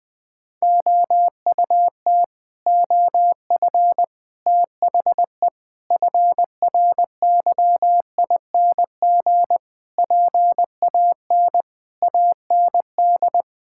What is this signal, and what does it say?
Morse code, heard as a keyed tone, about 20 wpm, 700 Hz. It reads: OUT OF THE FRYING PAN AND
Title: Sets of 9 Words - 20wpm